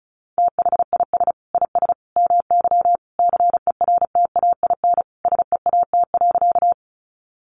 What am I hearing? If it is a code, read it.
Morse code, heard as a keyed tone, about 34 wpm, 700 Hz. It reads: THIS IS MY CERTAIN SEAT.
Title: Sentences from Top 1500 Words 35wpm